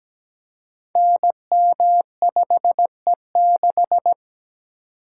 Morse code, heard as a keyed tone, about 17 wpm, 700 Hz. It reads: NM5E6
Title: Code Groups - 5 Letter:Numbers - 17wpm